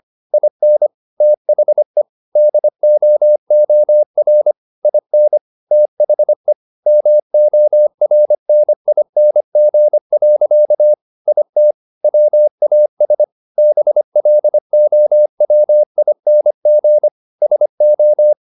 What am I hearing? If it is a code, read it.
Morse code, heard as a keyed tone, about 25 wpm, 600 Hz. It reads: IN THE DOOR IN THE MORNING. IT WAS BLOWING SO